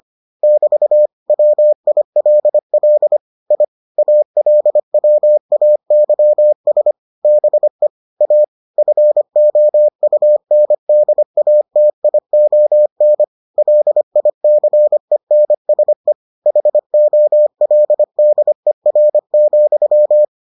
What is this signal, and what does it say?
Morse code, heard as a keyed tone, about 25 wpm, 600 Hz. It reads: <BT> WILL I ALWAYS BE A FOUNDATION LICENSE HOLDER,